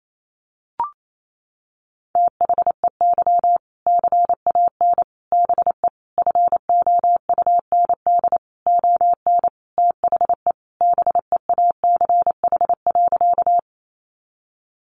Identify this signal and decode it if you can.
Morse code, heard as a keyed tone, about 28 wpm, 700 Hz. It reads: THEY CAN BE FOUND ON THE BEACH.
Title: Sentences from Top 1500 Words 28wpm